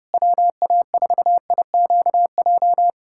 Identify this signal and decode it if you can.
Morse code, heard as a keyed tone, about 30 wpm, 690 Hz. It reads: WA4IQJ
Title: USA Callsign DE Callsign - 6 Characters - 30wpm